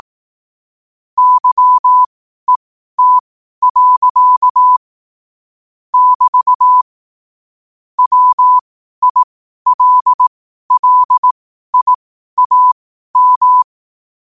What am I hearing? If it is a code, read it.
Morse code, heard as a keyed tone, about 18 wpm, spaced as for 12 wpm, 1000 Hz. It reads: YET. <BT> WILLIAM
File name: qotdfiles / 12wpm